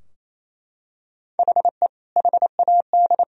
Morse code, heard as a keyed tone, about 28 wpm, 700 Hz. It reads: HE HAD
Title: Sentences from Top 1000 Words - Review Part 3 - 28wpm